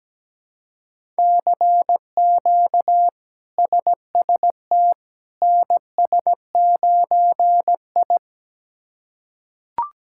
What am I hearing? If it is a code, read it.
Morse code, heard as a keyed tone, about 17 wpm, 700 Hz. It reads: CQ SST NS9I
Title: K1USN SST QSO Practice - 17wpm